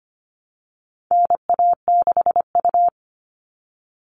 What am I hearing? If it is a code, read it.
Morse code, heard as a keyed tone, about 25 wpm, 700 Hz. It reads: NA6U
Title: All Cut Numbers - 4-Digit 25wpm